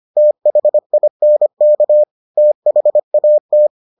Morse code, heard as a keyed tone, about 25 wpm, 600 Hz. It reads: THINK THAT